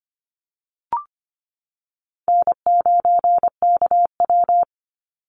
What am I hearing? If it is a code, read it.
Morse code, heard as a keyed tone, about 25 wpm, 700 Hz. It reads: N9KW
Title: NA SOTA Chaser Callsigns 25wpm